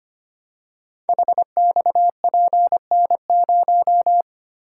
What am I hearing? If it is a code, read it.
Morse code, heard as a keyed tone, about 25 wpm, 700 Hz. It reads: HXPN0
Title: Code Groups - 5 Letter:Numbers - 25wpm